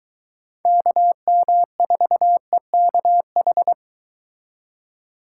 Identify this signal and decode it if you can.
Morse code, heard as a keyed tone, about 23 wpm, 700 Hz. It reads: KM4EKH